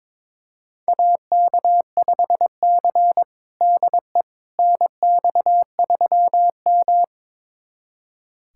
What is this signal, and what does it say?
Morse code, heard as a keyed tone, about 22 wpm, 700 Hz. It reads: AK5C DE NX3M
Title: USA Callsign DE Callsign - 4 Characters - 22wpm